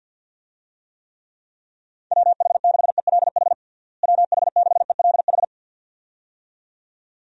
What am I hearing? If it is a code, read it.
Morse code, heard as a keyed tone, about 50 wpm, 700 Hz. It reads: WH6ELH WH6ELH
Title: USA Call Signs - 6 Characters - Repeated 2x - 50wpm